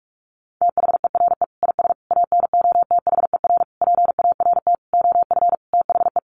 Morse code, heard as a keyed tone, about 44 wpm, 700 Hz. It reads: THERE IS ANOTHER PART OF THE
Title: Sentences from Top 1300 Words 45wpm